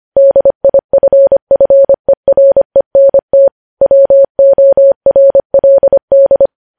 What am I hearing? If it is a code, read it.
Morse code, heard as a keyed tone, about 25 wpm, 550 Hz. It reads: DIFFERENT WORLD
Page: News Headlines in Morse Code at 25 WPM